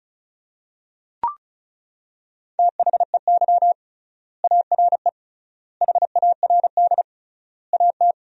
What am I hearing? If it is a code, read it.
Morse code, heard as a keyed tone, about 36 wpm, 700 Hz. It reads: THEY ARE HARD AT
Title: 3x Word-Spacing - Sentences from Top 300 Words - 35wpm